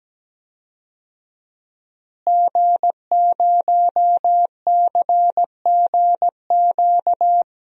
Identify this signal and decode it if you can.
Morse code, heard as a keyed tone, about 17 wpm, 700 Hz. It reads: G0CGQ
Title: DXCC-100 Call Signs - 5 Characters - Repeated 3x - 17wpm